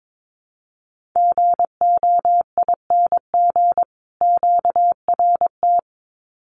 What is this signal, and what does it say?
Morse code, heard as a keyed tone, about 22 wpm, 700 Hz. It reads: GOING QRT
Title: Common QSO Phrases 22wpm